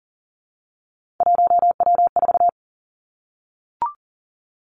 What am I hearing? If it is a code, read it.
Morse code, heard as a keyed tone, about 40 wpm, 700 Hz. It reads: 1W4
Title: Three Challenging Letter-Number Combinations 40wpm